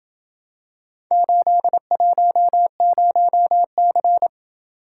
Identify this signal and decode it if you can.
Morse code, heard as a keyed tone, about 28 wpm, 700 Hz. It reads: 810C